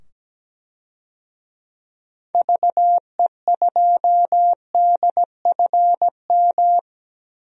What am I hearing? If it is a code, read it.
Morse code, heard as a keyed tone, about 17 wpm, 700 Hz. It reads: VE2DFM
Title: DXCC-100 Call Signs - 6 Characters - 17wpm